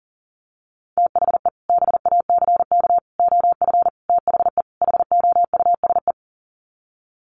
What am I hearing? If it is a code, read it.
Morse code, heard as a keyed tone, about 37 wpm, 700 Hz. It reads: THE BACK OF THE HOUSE
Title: Sets of 5 Words Encore2 40wpm